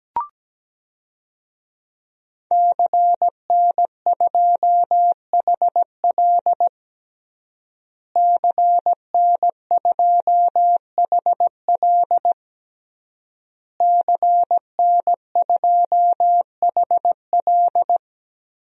Morse code, heard as a keyed tone, about 17 wpm, 700 Hz. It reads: CN2HL CN2HL CN2HL
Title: DXCC-100 Call Signs - 5 Characters - Repeated 3x - 17wpm